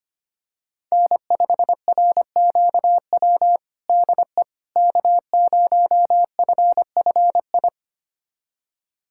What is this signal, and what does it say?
Morse code, heard as a keyed tone, about 25 wpm, 700 Hz. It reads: N5RQW DE K0FFI
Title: USA Callsign DE Callsign 25wpm